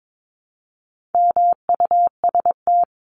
Morse code, heard as a keyed tone, about 22 wpm, 700 Hz. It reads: MUST
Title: Top 500 Words 22wpm